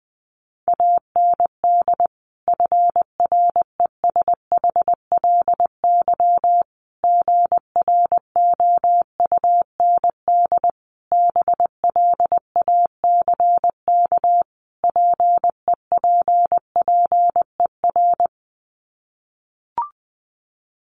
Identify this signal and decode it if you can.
Morse code, heard as a keyed tone, about 20 wpm, 700 Hz. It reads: AND FRESHLY GROUND BLACK PEPPER
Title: Sets of 5 Words Encore 20wpm